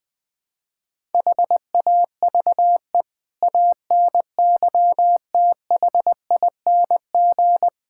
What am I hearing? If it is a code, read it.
Morse code, heard as a keyed tone, about 20 wpm, 700 Hz. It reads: HAVE ANYTHING
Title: Build A Phrase - 5-Word Phrases - 20wpm